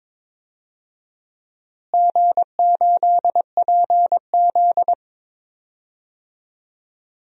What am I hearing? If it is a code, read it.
Morse code, heard as a keyed tone, about 21 wpm, 700 Hz. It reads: G8PZ
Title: DXCC-100 Call Signs - 4 Characters - Repeated 2x - 22wpm